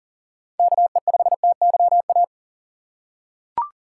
Morse code, heard as a keyed tone, about 40 wpm, 700 Hz. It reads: KE5TYA
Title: USA Call Signs - 4x Character Spacing - 40wpm